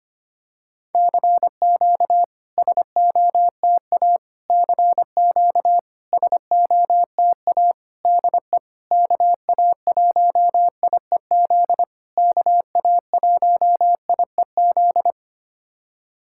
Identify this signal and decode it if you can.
Morse code, heard as a keyed tone, about 25 wpm, 700 Hz. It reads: CQ SOTA CQ SOTA DE KA1IEZ KA1IEZ
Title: SOTA QSO Practice 25wpm